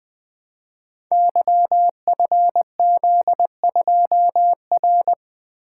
Morse code, heard as a keyed tone, about 20 wpm, 700 Hz. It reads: YFZ2R
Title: Code Groups - 5 Letter:Numbers - 20wpm